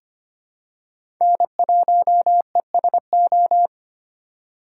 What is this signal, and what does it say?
Morse code, heard as a keyed tone, about 25 wpm, 700 Hz. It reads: N1ESO
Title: USA Call Signs - 5 Characters - 3x Character Spacing - 25wpm